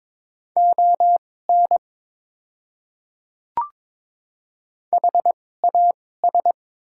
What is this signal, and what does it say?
Morse code, heard as a keyed tone, about 22 wpm, spaced as for 16 wpm, 690 Hz. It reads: ON HAS
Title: 2x Character Spacing - Top 200 Words - 22wpm